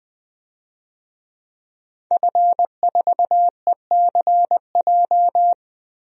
Morse code, heard as a keyed tone, about 20 wpm, 700 Hz. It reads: F4ECJ
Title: DXCC-100 Call Signs - 5 Characters - Repeated 2x - 20wpm